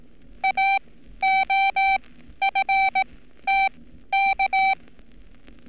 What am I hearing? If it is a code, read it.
Morse code, heard as a keyed tone, about 18 wpm, spaced as for 12 wpm, 780 Hz. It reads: AOFTK
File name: training / NAVPERS 11224-11235 / NAVPERS 11227 RA RB